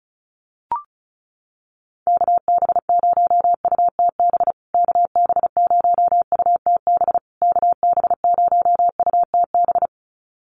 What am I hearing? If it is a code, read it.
Morse code, heard as a keyed tone, about 34 wpm, 700 Hz. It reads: KB0UTB KB0UTB KB0UTB
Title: USA Call Signs Repeated 3x 35wpm